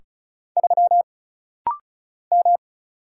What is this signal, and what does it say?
Morse code, heard as a keyed tone, about 36 wpm, 700 Hz. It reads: 3 M
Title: Single Letter-Number Alt - Rapid-Fire 35wpm